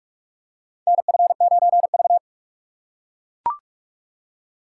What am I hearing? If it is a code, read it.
Morse code, heard as a keyed tone, about 42 wpm, 700 Hz. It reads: NF9V
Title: USA Call Signs - 4 Characters - 6x Character Spacing - 45wpm